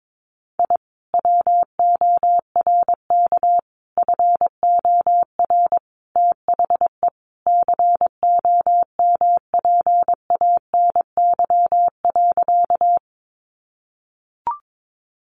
Morse code, heard as a keyed tone, about 22 wpm, 700 Hz. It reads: I WORK FOR THE COMPANY.
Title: Sentences from Top 600 Words 22wpm